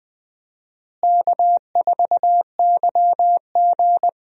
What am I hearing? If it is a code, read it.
Morse code, heard as a keyed tone, about 20 wpm, 700 Hz. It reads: K4YG
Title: USA Call Signs - 4 Characters - Repeated 2x - 20wpm